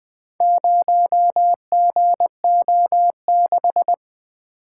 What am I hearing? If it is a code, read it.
Morse code, heard as a keyed tone, about 25 wpm, 700 Hz. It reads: 0GO6
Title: Speed-Racing - Four Letter-Number Combinations - 20wpm